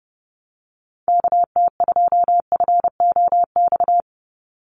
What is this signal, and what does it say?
Morse code, heard as a keyed tone, about 30 wpm, 700 Hz. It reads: KT2FOX
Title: USA Call Signs 30wpm